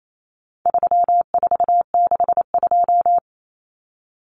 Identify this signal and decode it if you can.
Morse code, heard as a keyed tone, about 28 wpm, 700 Hz. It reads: 3462
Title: Common Cut Numbers - 4-Digit 28wpm